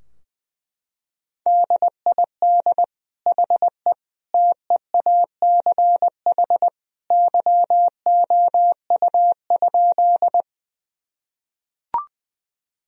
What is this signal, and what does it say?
Morse code, heard as a keyed tone, about 21 wpm, 700 Hz. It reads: DID HE TEACH YOU?
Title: Sentences from Top 1000 Words - Review Part 2 - 20wpm